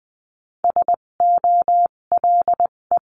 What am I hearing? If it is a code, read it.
Morse code, heard as a keyed tone, about 20 wpm, spaced as for 17 wpm, 700 Hz. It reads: SOLE
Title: 257 Lesson 42b - Words 14-20wpm